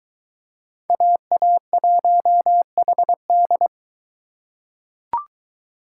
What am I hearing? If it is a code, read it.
Morse code, heard as a keyed tone, about 23 wpm, 700 Hz. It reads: AA1HD